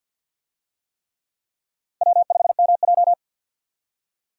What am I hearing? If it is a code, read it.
Morse code, heard as a keyed tone, about 45 wpm, 700 Hz. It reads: W5MJ
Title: USA Call Signs - 4 Characters - 50wpm